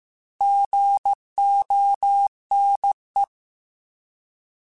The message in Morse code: GONE

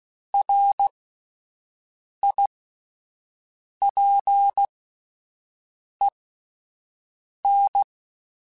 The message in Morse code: RIPEN

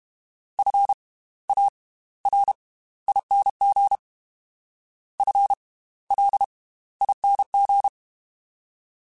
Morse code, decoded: FARING FLING

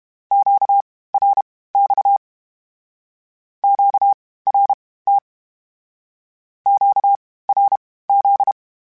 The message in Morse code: QRX QRT QRZ

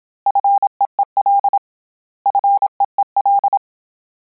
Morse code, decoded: FEEL FEEL

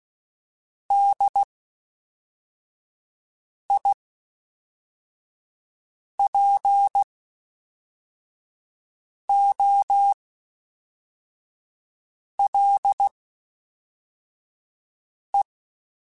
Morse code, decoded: DIPOLE

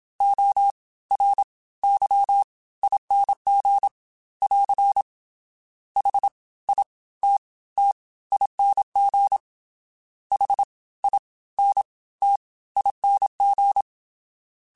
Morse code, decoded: ORYING <AR> HITTING HINTING

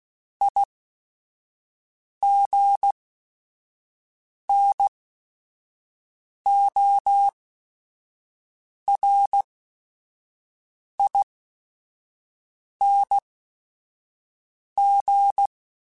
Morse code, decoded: IGNORING

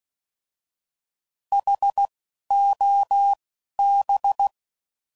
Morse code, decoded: H O B

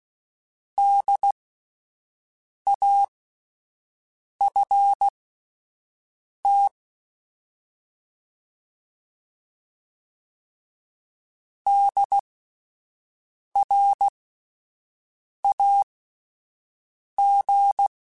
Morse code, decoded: DAFT DRAG

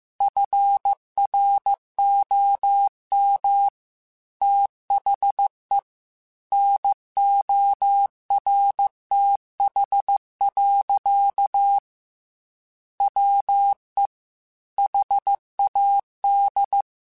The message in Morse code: FROM THE NORTH. WE HAD

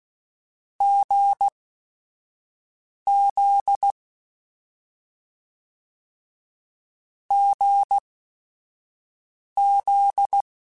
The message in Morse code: GZ GZ

